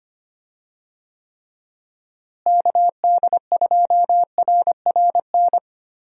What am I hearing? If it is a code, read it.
Morse code, heard as a keyed tone, about 25 wpm, 700 Hz. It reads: KD2RRN
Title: USA Call Signs - 6 Characters - Repeated 3x - 25wpm